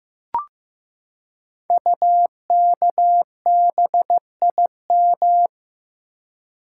Morse code, heard as a keyed tone, about 15 wpm, 700 Hz. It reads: UKBIM
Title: Code Groups - 5 Letters - 15wpm